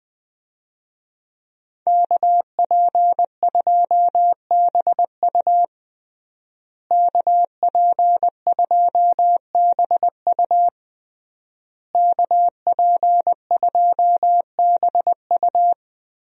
Morse code, heard as a keyed tone, about 21 wpm, 700 Hz. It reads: KP2BU KP2BU KP2BU
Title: USA Call Signs - 5 Characters - Repeated 3x - 20wpm